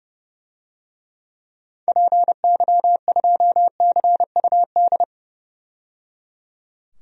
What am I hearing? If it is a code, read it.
Morse code, heard as a keyed tone, about 30 wpm, 700 Hz. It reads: PY2CUD
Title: DXCC-100 Call Signs - 6 Characters - 30wpm